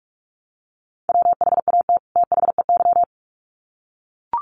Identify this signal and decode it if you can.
Morse code, heard as a keyed tone, about 44 wpm, 700 Hz. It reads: WHAT THEY
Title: Sets of 2 Words 45wpm